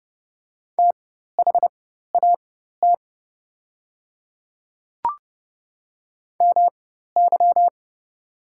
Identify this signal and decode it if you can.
Morse code, heard as a keyed tone, about 31 wpm, spaced as for 14 wpm, 700 Hz. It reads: THAT MY
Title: 4x Character Spacing - Top 100 Words - 30wpm